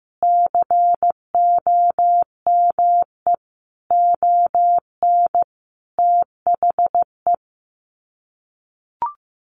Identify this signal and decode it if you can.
Morse code, heard as a keyed tone, about 15 wpm, 700 Hz. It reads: COME ON THE
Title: Sets of 5 Words Encore2 15wpm